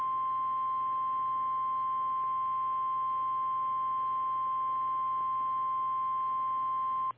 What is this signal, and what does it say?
Emergency Alert System (SAME) tones (the 1050 Hz attention tone). It is heard from 1.1 to 6.2 s.
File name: eas / KKPRKXPN / KKPRKXPN